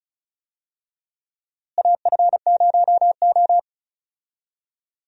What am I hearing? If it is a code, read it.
Morse code, heard as a keyed tone, about 36 wpm, 700 Hz. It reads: AF0O
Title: USA Call Signs - 4 Characters - Repeated 3x - 35wpm